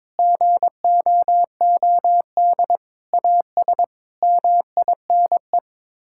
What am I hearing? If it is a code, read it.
Morse code, heard as a keyed tone, about 22 wpm, 700 Hz. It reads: GOOD AS MINE
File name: practice / 22wpm / Sets of 7 Words - 22wpm